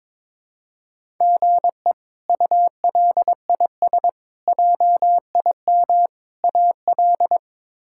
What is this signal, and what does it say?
Morse code, heard as a keyed tone, about 22 wpm, 700 Hz. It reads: GE ULIS JIM AL
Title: K1USN SST QSO Practice - 22wpm